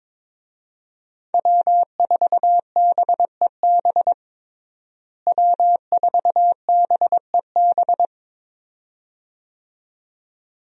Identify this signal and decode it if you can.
Morse code, heard as a keyed tone, about 21 wpm, 700 Hz. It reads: W4BEB W4BEB
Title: USA Call Signs - 5 Characters - Repeated 2x - 22wpm